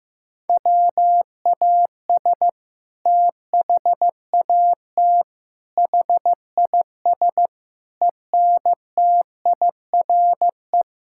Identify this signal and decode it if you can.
Morse code, heard as a keyed tone, about 15 wpm, 700 Hz. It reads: WAS THAT HIS ENTIRE
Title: Sentences from Top 1000 Words - Review Part 5 - 15wpm